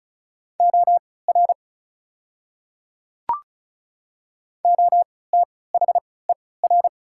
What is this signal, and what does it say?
Morse code, heard as a keyed tone, about 36 wpm, spaced as for 20 wpm, 690 Hz. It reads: OR OTHER
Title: 3x Character Spacing - Top 100 Words - 35wpm